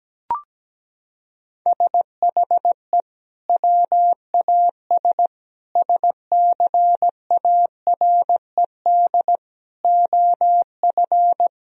Morse code, heard as a keyed tone, about 17 wpm, 700 Hz. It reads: SHE WAS SCARED OF
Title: Sentences from Top 1000 Words - Review Part 4 - 17wpm